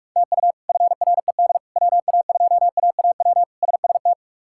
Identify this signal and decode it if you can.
Morse code, heard as a keyed tone, about 42 wpm, 700 Hz. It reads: TU FRED WA2AAW SST
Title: K1USN SST QSO Practice - 45wpm